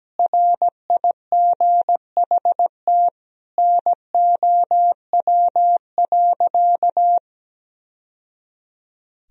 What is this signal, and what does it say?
Morse code, heard as a keyed tone, about 17 wpm, 700 Hz. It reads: RIGHT NOW.
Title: Sentences from Top 1000 Words - Review Part 4 - 17wpm